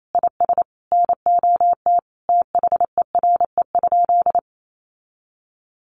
Morse code, heard as a keyed tone, about 28 wpm, 700 Hz. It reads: IS NOT THERE?
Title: Sentences from Top 600 Words 28wpm